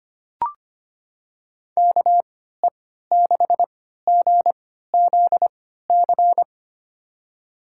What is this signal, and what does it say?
Morse code, heard as a keyed tone, about 25 wpm, spaced as for 14 wpm, 700 Hz. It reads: KE6GZC
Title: USA Call Signs - 6 Characters - 3x Character Spacing - 25wpm